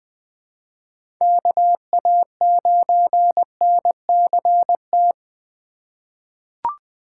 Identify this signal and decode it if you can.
Morse code, heard as a keyed tone, about 20 wpm, 700 Hz. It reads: KA9NCT
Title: USA Call Signs - 6 Characters - 2x Character Spacing - 20wpm